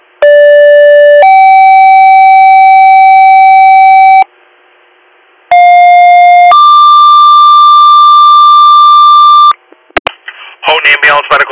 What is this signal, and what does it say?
Two-tone page, heard as a one-second tone, then a three-second tone, about 580 then 770 Hz, about 710 then 1180 Hz, with speech. It is heard from 0.2 to 9.5 s.